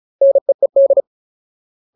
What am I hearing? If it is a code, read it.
Morse code, heard as a keyed tone, about 34 wpm, 550 Hz. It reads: NEED
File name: Ultimate Word List / 35WPM Ultimate